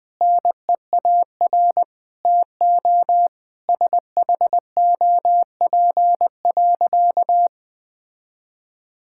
Morse code, heard as a keyed tone, about 20 wpm, 700 Hz. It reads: NEAR TO SHOP.
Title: Sentences from Top 1600 Words 20wpm